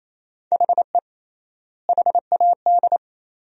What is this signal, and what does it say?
Morse code, heard as a keyed tone, about 28 wpm, 700 Hz. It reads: HE HAD
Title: 3x Word-Spacing - Sentences from Top 300 Words - 28wpm